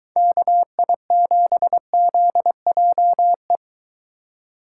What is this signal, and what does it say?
Morse code, heard as a keyed tone, about 21 wpm, 700 Hz. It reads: KI7ZJE